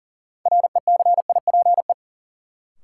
Morse code, heard as a keyed tone, about 40 wpm, 700 Hz. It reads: RECIPE
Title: Sentences from Top 1200 Words - New Word Drill - 40wpm